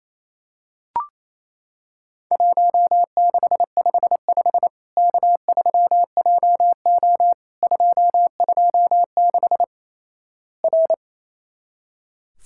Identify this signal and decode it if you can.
Morse code, heard as a keyed tone, about 28 wpm, 700 Hz. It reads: 1655 K3JO 226 R
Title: Worked All Europe DX Contest - QTC Exchange - 28wpm